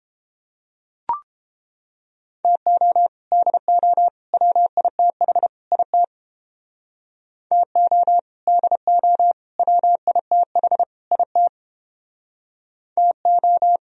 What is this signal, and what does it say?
Morse code, heard as a keyed tone, about 29 wpm, 700 Hz. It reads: TO DO WITH IT TO DO WITH IT TO